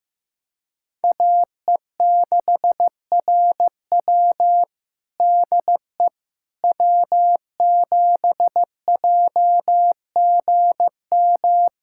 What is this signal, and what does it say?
Morse code, heard as a keyed tone, about 15 wpm, 700 Hz. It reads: AE6RW DE W7JGM
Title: USA Callsign DE Callsign - 5 Characters - 15wpm